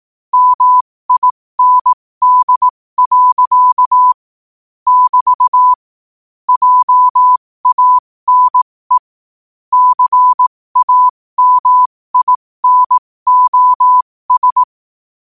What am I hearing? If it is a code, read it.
Morse code, heard as a keyed tone, about 18 wpm, spaced as for 15 wpm, 1000 Hz. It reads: MIND. <BT> JANE CAMINOS
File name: qotdfiles / 15wpm